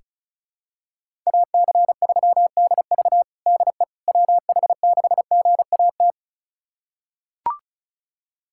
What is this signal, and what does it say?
Morse code, heard as a keyed tone, about 36 wpm, 700 Hz. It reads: AC3DV DE WH6GAT
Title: USA Callsign DE Callsign 35wpm